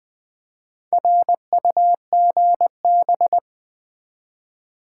Morse code, heard as a keyed tone, about 20 wpm, 690 Hz. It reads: RUGB
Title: Build A Word - 1 Letter At A Time - 5-Letter Words - 20wpm